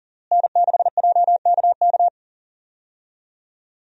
Morse code, heard as a keyed tone, about 40 wpm, 700 Hz. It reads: N6JKK
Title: USA Call Signs - 5 Characters - Repeated 3x - 40wpm